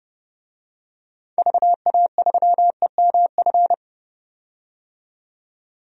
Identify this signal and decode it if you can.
Morse code, heard as a keyed tone, about 30 wpm, 690 Hz. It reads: VA3EMF
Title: DXCC-100 Call Signs - 6 Characters - 30wpm